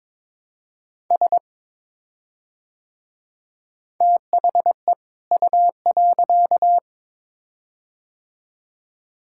Morse code, heard as a keyed tone, about 22 wpm, 700 Hz. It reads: S THE U.
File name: Build A Phrase - 2-Word Phrases - 22wpm